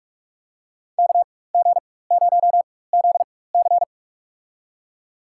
Morse code, heard as a keyed tone, about 46 wpm, spaced as for 21 wpm, 700 Hz. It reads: KG0ZC
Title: USA Call Signs - 5 Characters - 4x Character Spacing - 45wpm